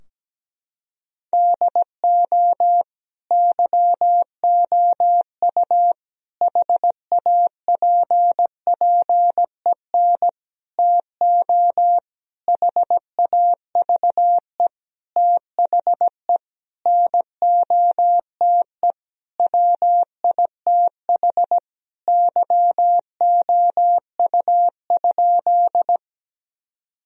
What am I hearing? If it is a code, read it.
Morse code, heard as a keyed tone, about 17 wpm, 700 Hz. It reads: DO YOU HAPPEN TO HAVE THE NOTE WITH YOU?
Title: Sentences from Top 1000 Words - Review Part 2 - 17wpm